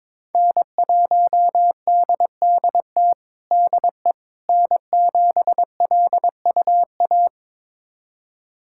Morse code, heard as a keyed tone, about 22 wpm, 700 Hz. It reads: N1DDT DE N7LUA
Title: USA Callsign DE Callsign - 5 Characters - 22wpm